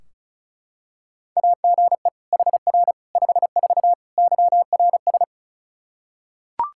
Morse code, heard as a keyed tone, about 34 wpm, 700 Hz. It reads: AGE HR 54 YRS
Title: Practice QSO - Exchange 3 - More Info - 35wpm